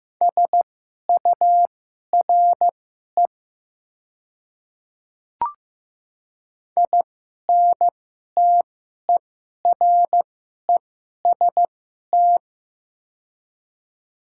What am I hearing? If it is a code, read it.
Morse code, heard as a keyed tone, about 15 wpm, spaced as for 11 wpm, 690 Hz. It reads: SURE INTEREST
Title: 2x Character Spacing - Top 500 Words - 15wpm